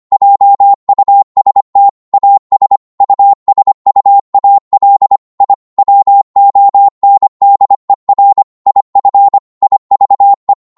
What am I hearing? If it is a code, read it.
Morse code, heard as a keyed tone, about 25 wpm, 800 Hz. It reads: JUST AS USUAL I WONDER IF IVE